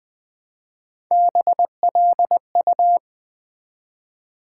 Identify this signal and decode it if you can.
Morse code, heard as a keyed tone, about 20 wpm, 700 Hz. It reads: BLU
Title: Build A Word - 1 Letter At A Time - 4-Letter Words - 20wpm